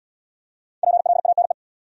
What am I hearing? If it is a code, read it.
Morse code, heard as a keyed tone, about 65 wpm, 700 Hz. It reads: PLANE